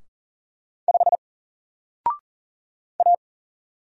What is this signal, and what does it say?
Morse code, heard as a keyed tone, about 38 wpm, 700 Hz. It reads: 5 A
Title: Single Letter-Number Alt 40wpm